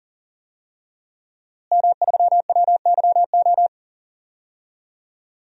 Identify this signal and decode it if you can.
Morse code, heard as a keyed tone, about 38 wpm, 700 Hz. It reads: M3WYO
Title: DXCC-100 Call Signs - 5 Characters - Repeated 3x - 40wpm